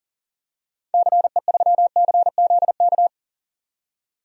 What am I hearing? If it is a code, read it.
Morse code, heard as a keyed tone, about 40 wpm, 700 Hz. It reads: CE3CZK
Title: DXCC-100 Call Signs - 6 Characters - 40wpm